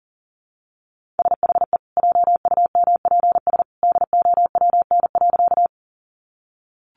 Morse code, heard as a keyed tone, about 40 wpm, 700 Hz. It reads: SHE JUMPS DOWN.
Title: Easy Phrases 40wpm